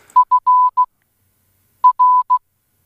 Morse code, heard as a keyed tone, about 16 wpm, 1000 Hz. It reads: F R